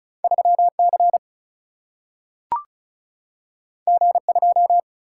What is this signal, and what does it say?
Morse code, heard as a keyed tone, about 36 wpm, 700 Hz. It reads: 3 C G 2
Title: Single-Letter Single-Number 35wpm